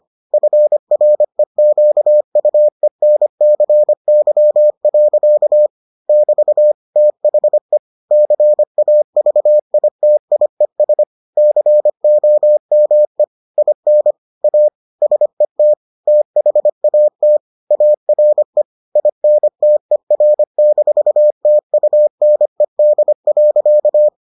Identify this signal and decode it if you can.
Morse code, heard as a keyed tone, about 25 wpm, 600 Hz. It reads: FREQUENCY. <BT> THE CAVITIES COME IN A SET THAT ARE INTER-TUNED.